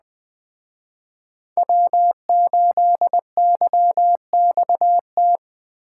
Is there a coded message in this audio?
Yes: Morse code from 1.6 to 5.4 s, heard as a keyed tone, about 22 wpm, 700 Hz. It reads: W8YXT